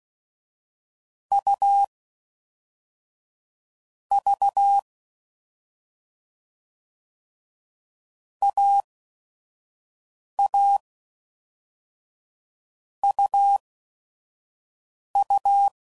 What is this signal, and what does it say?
Morse code, heard as a keyed tone, about 16 wpm, spaced as for 5 wpm, 800 Hz. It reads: U V AA UU
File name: Combinations / Difficult Combos / AUV AUV4